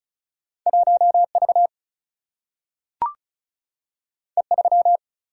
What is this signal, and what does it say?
Morse code, heard as a keyed tone, about 36 wpm, 700 Hz. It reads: 1V E3
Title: Single-Letter Single-Number 35wpm